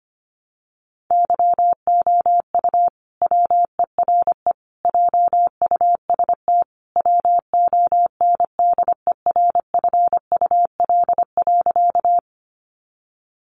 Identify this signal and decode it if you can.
Morse code, heard as a keyed tone, about 25 wpm, 700 Hz. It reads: YOU WERE JUST WONDERFUL.
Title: Sentences from Top 1100 Words 25wpm